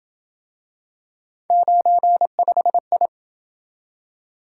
Morse code, heard as a keyed tone, about 27 wpm, 700 Hz. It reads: 95I